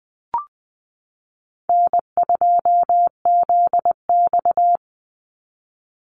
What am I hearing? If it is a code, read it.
Morse code, heard as a keyed tone, about 20 wpm, 700 Hz. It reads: N2ZX
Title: Easy Call Signs Repeated 3x - 20wpm